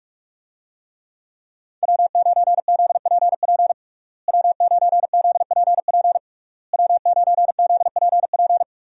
Morse code, heard as a keyed tone, about 46 wpm, 700 Hz. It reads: W9ZPP W9ZPP W9ZPP
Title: USA Call Signs - 5 Characters - Repeated 3x - 45wpm